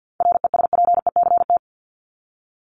Morse code, heard as a keyed tone, about 50 wpm, 700 Hz. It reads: RESPECT
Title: Top 1501-1750 Words 50wpm